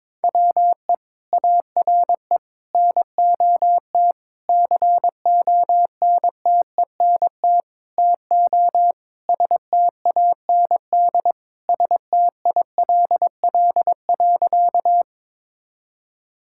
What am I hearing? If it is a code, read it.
Morse code, heard as a keyed tone, about 22 wpm, 700 Hz. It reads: WE ARE NOT CONTENT TO STAND STILL.
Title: Sentences from Top 1000 Words - Review Part 6 - 22wpm